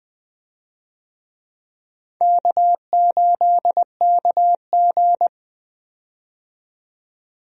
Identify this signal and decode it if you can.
Morse code, heard as a keyed tone, about 20 wpm, 700 Hz. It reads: K8KG
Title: USA Call Signs - 4 Characters - 20wpm